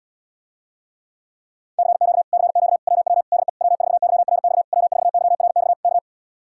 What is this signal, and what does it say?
Morse code, heard as a keyed tone, about 77 wpm, 700 Hz. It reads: CQ CQ WW DE W4YRX W4YRX K